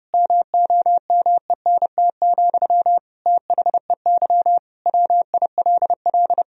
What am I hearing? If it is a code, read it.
Morse code, heard as a keyed tone, about 32 wpm, 700 Hz. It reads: MOMENT, THEY WILL
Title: Sentences from Top 1000 Words - Review Part 2 - 30wpm